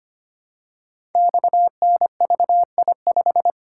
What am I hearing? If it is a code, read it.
Morse code, heard as a keyed tone, about 25 wpm, 700 Hz. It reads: XNVI5
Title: Code Groups - 5 Letter:Numbers - 25wpm